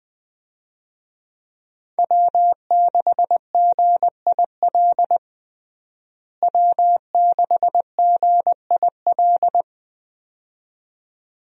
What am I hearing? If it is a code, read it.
Morse code, heard as a keyed tone, about 20 wpm, 700 Hz. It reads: W6GIL W6GIL
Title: USA Call Signs - 5 Characters - Repeated 2x - 20wpm